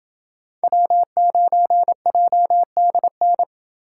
Morse code, heard as a keyed tone, about 27 wpm, 700 Hz. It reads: W9JDN